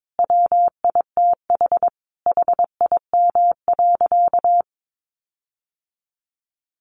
Morse code, heard as a keyed tone, about 22 wpm, 700 Hz. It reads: WITH HIM.
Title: Sentences from Top 1400 Words 22wpm